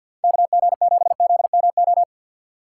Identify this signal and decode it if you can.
Morse code, heard as a keyed tone, about 50 wpm, 700 Hz. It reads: KG7ZMO
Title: USA Call Signs - 2x Character Spacing - 50wpm